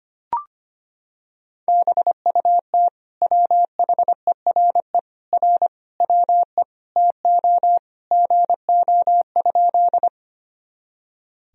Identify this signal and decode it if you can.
Morse code, heard as a keyed tone, about 25 wpm, 700 Hz. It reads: BUT WHERE R WE TO GO?
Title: Sentences from Top 200 Words and Abbreviations 25wpm